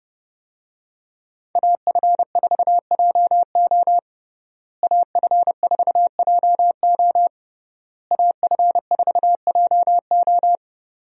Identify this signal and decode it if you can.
Morse code, heard as a keyed tone, about 30 wpm, 700 Hz. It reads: AF4JO AF4JO AF4JO
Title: USA Call Signs - 5 Characters - Repeated 3x - 30wpm